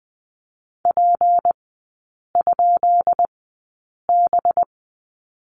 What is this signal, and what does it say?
Morse code, heard as a keyed tone, about 19 wpm, spaced as for 8 wpm, 700 Hz. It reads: P?B
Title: 184 Lesson 30e - Letter b - Three Characters 20wpm